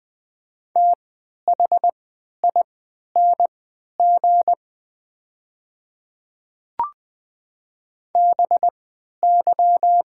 Morse code, heard as a keyed tone, about 21 wpm, spaced as for 12 wpm, 700 Hz. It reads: THING BY
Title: 3x Character Spacing - Top 100 Words - 20wpm